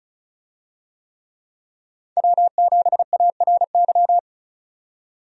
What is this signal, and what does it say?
Morse code, heard as a keyed tone, about 36 wpm, 700 Hz. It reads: W7ARY
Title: USA Call Signs - 5 Characters - 35wpm